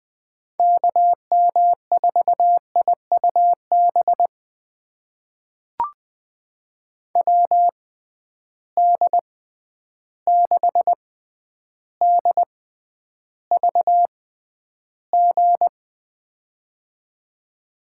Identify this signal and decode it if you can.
Morse code, heard as a keyed tone, about 21 wpm, spaced as for 10 wpm, 700 Hz. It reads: KM4IUB WD6DVG
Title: USA Call Signs - 6 Characters - 6x Character Spacing - 20wpm